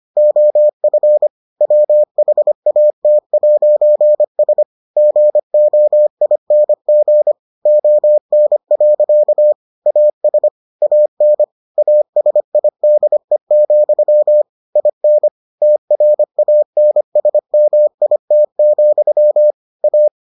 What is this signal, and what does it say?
Morse code, heard as a keyed tone, about 25 wpm, 600 Hz. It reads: OF WHAT'S GOING ON. AS AN ASIDE, IN TRANSMIT, A